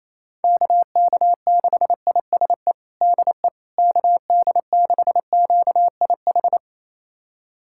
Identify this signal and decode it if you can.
Morse code, heard as a keyed tone, about 27 wpm, 700 Hz. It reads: KK6ISE DE KD6QIH
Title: USA Callsign DE Callsign - 6 Characters - 28wpm